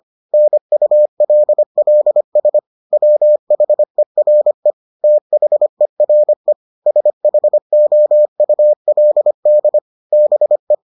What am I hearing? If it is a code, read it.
Morse code, heard as a keyed tone, about 25 wpm, 600 Hz. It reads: NULLS WHERE THERE SHOULD BE